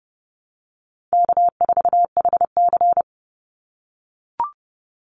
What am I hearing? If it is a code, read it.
Morse code, heard as a keyed tone, about 30 wpm, 700 Hz. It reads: K4HC
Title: Easy Call Signs - 30wpm